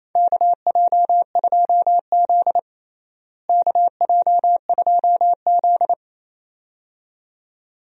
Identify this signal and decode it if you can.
Morse code, heard as a keyed tone, about 28 wpm, 700 Hz. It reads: KJ2Z KJ2Z
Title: USA Call Signs - 4 Characters - Repeated 2x - 28wpm